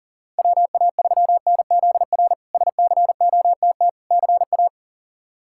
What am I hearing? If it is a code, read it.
Morse code, heard as a keyed tone, about 40 wpm, 700 Hz. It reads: WA3NZR SCOTT CA
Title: K1USN SST QSO Practice - 40wpm